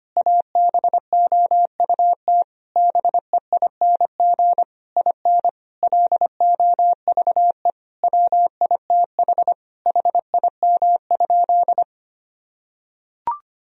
Morse code, heard as a keyed tone, about 25 wpm, 700 Hz. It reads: ABOUT BEING IN LOVE WITH HIM?
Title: Sentences from Top 1000 Words - Review Part 3 - 25wpm